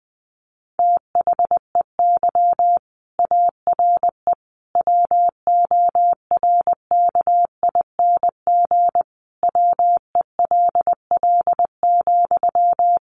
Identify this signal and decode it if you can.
Morse code, heard as a keyed tone, about 20 wpm, 700 Hz. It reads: THEY ARE WORKING WELL,
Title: Sentences from Top 700 Words 20wpm